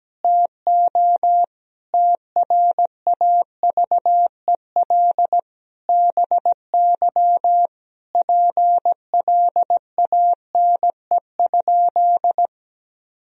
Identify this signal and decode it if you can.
Morse code, heard as a keyed tone, about 17 wpm, 700 Hz. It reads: TO TRAVEL BY PLANE?
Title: Sentences from Top 1000 Words - Review Part 3 - 17wpm